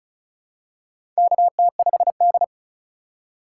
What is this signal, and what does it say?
Morse code, heard as a keyed tone, about 35 wpm, 700 Hz. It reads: KT5D
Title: USA Call Signs - 4 Characters - 3x Character Spacing - 35wpm